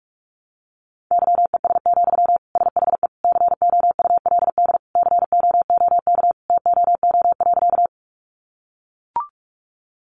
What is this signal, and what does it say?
Morse code, heard as a keyed tone, about 44 wpm, 700 Hz. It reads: YES, SHE COULD COOK TOO.
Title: Sentences from Top 700 Words 45wpm